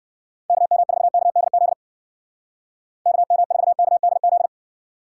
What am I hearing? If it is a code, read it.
Morse code, heard as a keyed tone, about 57 wpm, 700 Hz. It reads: KM4KDZ KM4KDZ
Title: Speed Racing - USA 2x3 Call Signs 50wpm